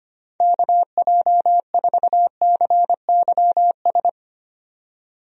Morse code, heard as a keyed tone, about 25 wpm, 700 Hz. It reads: KJ4CYS
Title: USA Call Signs - 6 Characters - 25wpm